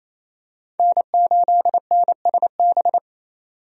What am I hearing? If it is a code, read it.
Morse code, heard as a keyed tone, about 27 wpm, 700 Hz. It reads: N8NSB
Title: USA Call Signs - 5 Characters - 28wpm